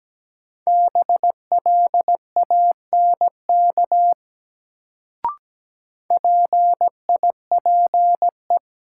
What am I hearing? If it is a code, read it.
Morse code, heard as a keyed tone, about 17 wpm, 700 Hz. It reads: BLANK PIPE
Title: Sentences from Top 1500 Words - New Word Drill - 17wpm